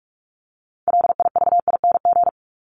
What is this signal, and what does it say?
Morse code, heard as a keyed tone, about 44 wpm, 700 Hz. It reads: LIVING
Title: ING Suffix - Top 100 Words with Base Words 45wpm